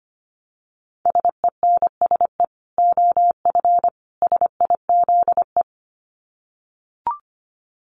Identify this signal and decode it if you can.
Morse code, heard as a keyed tone, about 25 wpm, 700 Hz. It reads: SENSE OF SIZE
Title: Dit Challenge 25wpm